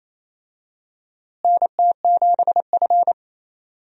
Morse code, heard as a keyed tone, about 28 wpm, 700 Hz. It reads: NT7F
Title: USA Call Signs - 4 Characters - Repeated 3x - 28wpm